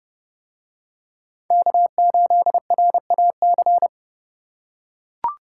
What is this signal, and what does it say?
Morse code, heard as a keyed tone, about 29 wpm, 700 Hz. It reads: K8RAC